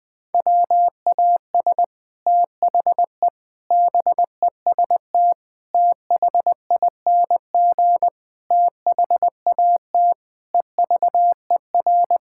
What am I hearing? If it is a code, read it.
Morse code, heard as a keyed tone, about 21 wpm, 700 Hz. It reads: WAS THE BEST THING THAT EVER
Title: Sets of 8 Words - 20wpm